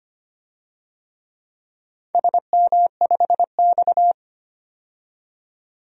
Morse code, heard as a keyed tone, about 25 wpm, 700 Hz. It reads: SM5X
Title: DXCC-100 Call Signs - 4 Characters - 25wpm